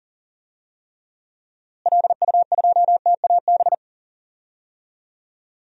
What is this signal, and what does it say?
Morse code, heard as a keyed tone, about 37 wpm, 700 Hz. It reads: LU2TAB
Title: DXCC-100 Call Signs - 6 Characters - Repeated 2x - 40wpm